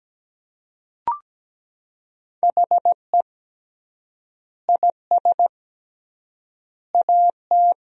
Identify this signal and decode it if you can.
Morse code, heard as a keyed tone, about 17 wpm, 700 Hz. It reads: HE IS AT
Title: 3x Word-Spacing - Sentences from Top 200 Words - 17wpm